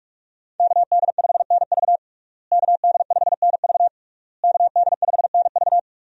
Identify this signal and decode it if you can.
Morse code, heard as a keyed tone, about 48 wpm, 700 Hz. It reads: KD5NV KD5NV KD5NV
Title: USA Call Signs - 5 Characters - Repeated 3x - 45wpm